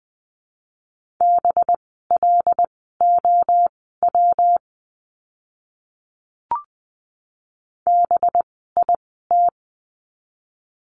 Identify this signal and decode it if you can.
Morse code, heard as a keyed tone, about 20 wpm, spaced as for 14 wpm, 700 Hz. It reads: BLOW BIT
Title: 185 Lesson 30f - Letter b - Words 20wpm